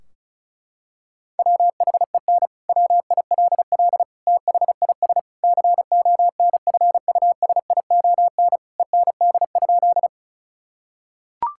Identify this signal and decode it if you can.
Morse code, heard as a keyed tone, about 36 wpm, 700 Hz. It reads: WHEN WILL THIS CONFUSION END?
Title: Sentences from Top 1600 Words 35wpm